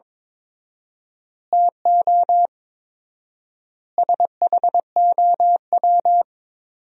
Morse code, heard as a keyed tone, about 22 wpm, 700 Hz. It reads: TO SHOW
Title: 4x Word-Spacing - Sentences from Top 200 Words - 22wpm